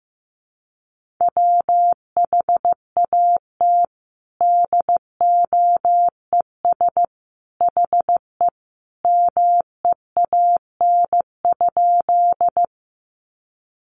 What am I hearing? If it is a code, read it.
Morse code, heard as a keyed tone, about 15 wpm, 700 Hz. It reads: WHAT DOES HE MEAN?
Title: Sentences from Top 200 Words 15wpm